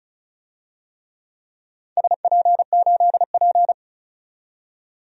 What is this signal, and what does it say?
Morse code, heard as a keyed tone, about 36 wpm, 700 Hz. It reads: SP8P
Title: DXCC-100 Call Signs - 4 Characters - 35wpm